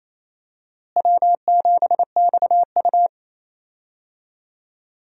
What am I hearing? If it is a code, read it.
Morse code, heard as a keyed tone, about 28 wpm, 700 Hz. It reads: W7XU
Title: USA Call Signs - 4 Characters - Repeated 2x - 28wpm